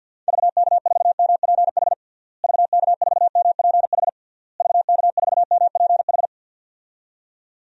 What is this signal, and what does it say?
Morse code, heard as a keyed tone, about 52 wpm, 700 Hz. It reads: VK4MPH VK4MPH VK4MPH
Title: DXCC-100 Call Signs - 6 Characters - Repeated 3x - 50wpm